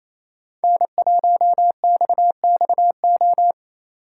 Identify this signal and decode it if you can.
Morse code, heard as a keyed tone, about 28 wpm, 700 Hz. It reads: N1XXO
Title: USA Call Signs - 5 Characters - 2x Character Spacing - 28wpm